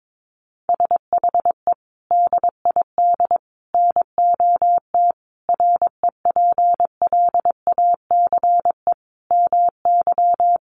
Morse code, heard as a keyed tone, about 22 wpm, 700 Hz. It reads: SHE DID NOT REPLACE MY
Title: Sentences from Top 1400 Words 22wpm